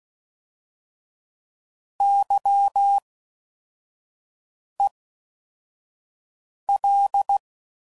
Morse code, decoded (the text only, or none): YEL